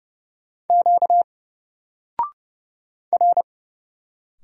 Morse code, heard as a keyed tone, about 31 wpm, 700 Hz. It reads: Q R
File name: Single-Letter-Number-30wpm